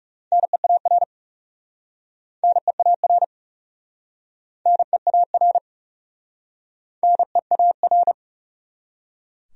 Morse code, heard as a keyed tone, about 38 wpm, 700 Hz. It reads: NEAR NEAR NEAR NEAR
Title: Speed-Racing Top 300 Words 30wpm